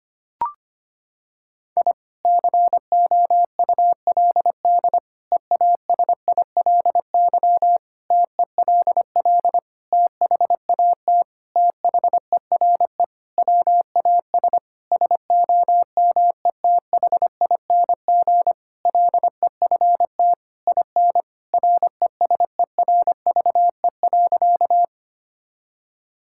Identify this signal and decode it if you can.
Morse code, heard as a keyed tone, about 25 wpm, 700 Hz. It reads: I COULD EASILY TELL THAT THERE WAS SOMETHING LEFT IN RESERVE.
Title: Sentences from Top 1000 Words - Review Part 6 - 25wpm